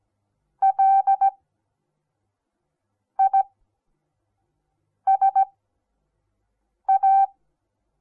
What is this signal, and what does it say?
Morse code, heard as a keyed tone, about 17 wpm, spaced as for 5 wpm, 780 Hz. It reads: LISA
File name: Onlinelehrgang / morsen / DJ4UF Morsen08